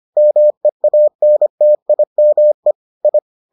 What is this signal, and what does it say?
Morse code, heard as a keyed tone, about 25 wpm, 600 Hz. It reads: MEANTIME I